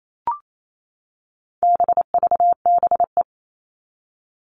Morse code, heard as a keyed tone, about 28 wpm, 700 Hz. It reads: BVBE
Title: All Cut Numbers - 4-Digit 28wpm